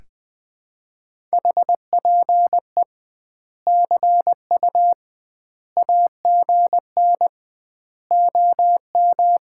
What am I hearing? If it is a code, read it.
Morse code, heard as a keyed tone, about 20 wpm, 700 Hz. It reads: HPE CU AGN OM
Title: Common QSO Phrases - 2x Word-Spacing - 20wpm